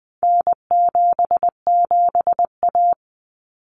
Morse code, heard as a keyed tone, about 20 wpm, 700 Hz. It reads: N77A
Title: Common Cut Numbers - 4-Digit 20wpm